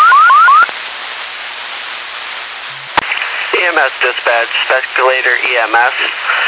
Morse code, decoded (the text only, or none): none